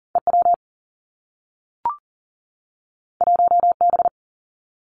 Morse code, heard as a keyed tone, about 38 wpm, 700 Hz. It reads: E W 1 B
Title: Two Challenging Letter-Number Combinations 40wpm